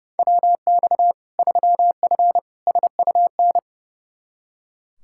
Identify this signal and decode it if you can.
Morse code, heard as a keyed tone, about 30 wpm, 700 Hz. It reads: WX 3F SUN
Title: Practice QSO - Exchange 2 - Info - 30wpm